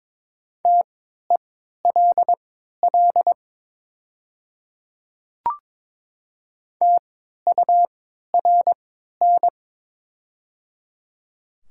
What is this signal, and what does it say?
Morse code, heard as a keyed tone, about 21 wpm, spaced as for 12 wpm, 700 Hz. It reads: TELL TURN
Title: 3x Character Spacing - Top 200 Words - 22wpm